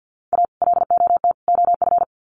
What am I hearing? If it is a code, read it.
Morse code, heard as a keyed tone, about 48 wpm, 700 Hz. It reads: A LOT OF
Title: Sets of 3 Words 50wpm